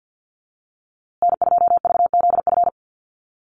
Morse code, heard as a keyed tone, about 50 wpm, 700 Hz. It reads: N2VZF
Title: USA Call Signs Repeated 3x 50wpm